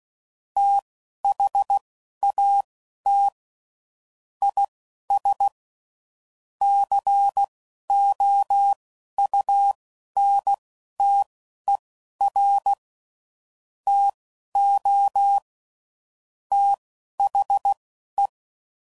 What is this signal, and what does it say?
Morse code, heard as a keyed tone, about 16 wpm, spaced as for 11 wpm, 800 Hz. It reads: THAT IS COUNTER TO THE
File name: CW Code Groups @10WPM / Technical Subjects / XL